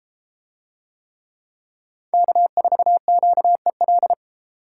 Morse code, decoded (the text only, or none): K4QEL